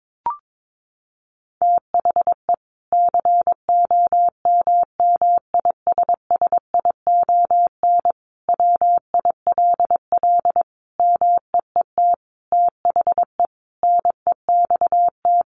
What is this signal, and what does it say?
Morse code, heard as a keyed tone, about 22 wpm, 700 Hz. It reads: THE COMMISSION WILL MEET THE NEXT
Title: Sentences from Top 1400 Words 22wpm